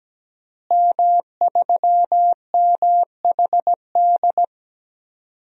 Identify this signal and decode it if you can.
Morse code, heard as a keyed tone, about 17 wpm, 700 Hz. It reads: M3MHD
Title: DXCC-100 Call Signs - 5 Characters - Repeated 3x - 17wpm